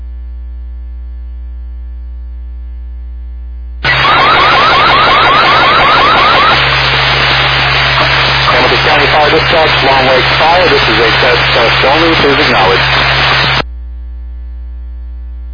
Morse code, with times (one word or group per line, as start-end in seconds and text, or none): none